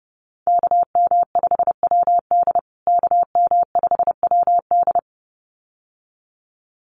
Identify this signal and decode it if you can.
Morse code, heard as a keyed tone, about 30 wpm, 700 Hz. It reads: KM5WD KM5WD
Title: USA Call Signs Repeated 2x 30wpm